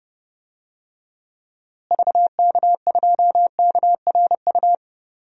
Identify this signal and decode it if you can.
Morse code, heard as a keyed tone, about 30 wpm, 700 Hz. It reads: VK2KRU
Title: DXCC-100 Call Signs - 6 Characters - Repeated 2x - 30wpm